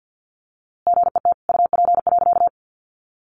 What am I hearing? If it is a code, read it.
Morse code, heard as a keyed tone, about 50 wpm, 700 Hz. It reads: GET UP.
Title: A Sound Of Thunder 50wpm